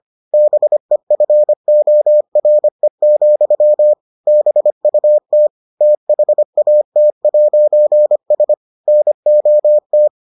Morse code, heard as a keyed tone, about 25 wpm, 600 Hz. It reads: BEFORE, BUT THAT'S NOT